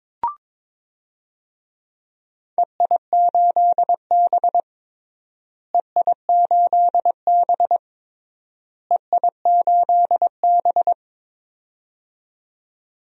Morse code, heard as a keyed tone, about 22 wpm, 700 Hz. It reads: EI8B EI8B EI8B
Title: DXCC-100 Call Signs - 4 Characters - Repeated 3x - 22wpm